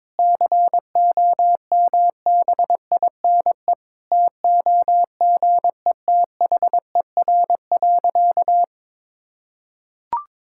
Morse code, heard as a keyed tone, about 22 wpm, 700 Hz. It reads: COMBINE TOGETHER.
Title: Sentences from Top 1600 Words 22wpm